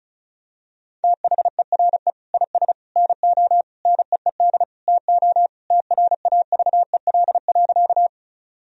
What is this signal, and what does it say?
Morse code, heard as a keyed tone, about 36 wpm, 700 Hz. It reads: THERE IS NO NEED TO TRAVEL.
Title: Sentences from Top 1000 Words - Review Part 2 - 35wpm